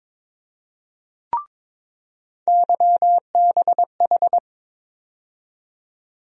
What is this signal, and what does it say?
Morse code, heard as a keyed tone, about 22 wpm, 700 Hz. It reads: YBH
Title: USA Call Sign Postfixes - 3 Letters 22wpm